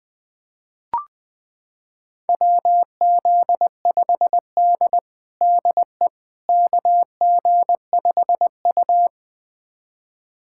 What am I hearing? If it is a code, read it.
Morse code, heard as a keyed tone, about 20 wpm, 700 Hz. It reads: WZ5D DE KG5U
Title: USA Callsign DE Callsign - 4 Characters - 20wpm